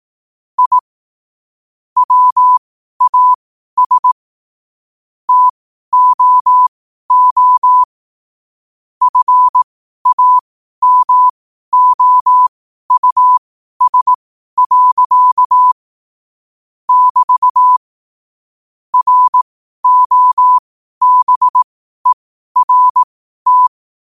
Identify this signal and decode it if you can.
Morse code, heard as a keyed tone, about 18 wpm, spaced as for 12 wpm, 1000 Hz. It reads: I WAS TOO FAMOUS. <BT> ROBERT
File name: qotdfiles / 12wpm